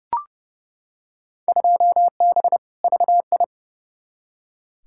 Morse code, heard as a keyed tone, about 30 wpm, 700 Hz. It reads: 2B VI
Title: Field Day Exchanges Encore - 30wpm